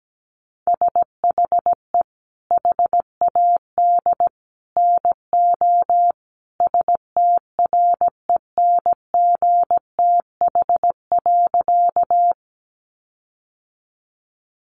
Morse code, heard as a keyed tone, about 17 wpm, 700 Hz. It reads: SHE HAD NO STRENGTH.
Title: Sentences from Top 1300 Words 17wpm